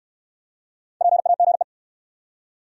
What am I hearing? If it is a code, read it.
Morse code, heard as a keyed tone, about 67 wpm, 700 Hz. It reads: PAGE